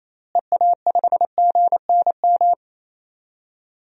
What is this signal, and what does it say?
Morse code, heard as a keyed tone, about 28 wpm, 700 Hz. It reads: EA5GNM
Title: DXCC-100 Call Signs - 6 Characters - Repeated 2x - 28wpm